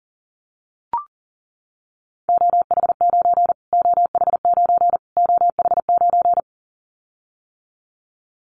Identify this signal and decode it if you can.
Morse code, heard as a keyed tone, about 40 wpm, 700 Hz. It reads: OH9 OH9 OH9
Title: Callsign Prefixes for Top 25 DXCC - Repeated 3x - 40wpm